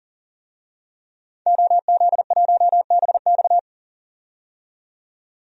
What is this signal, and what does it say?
Morse code, heard as a keyed tone, about 40 wpm, 700 Hz. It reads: OZ1BX
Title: DXCC-100 Call Signs - 5 Characters - 40wpm